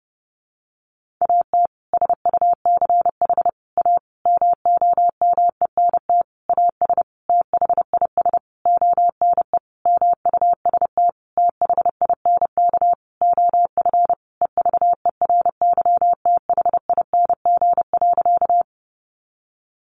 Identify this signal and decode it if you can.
Morse code, heard as a keyed tone, about 28 wpm, 700 Hz. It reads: AT SUCH A MOMENT AS THIS ONE MUST THINK OF EVERYTHING.
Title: Sentences from Top 1000 Words 30wpm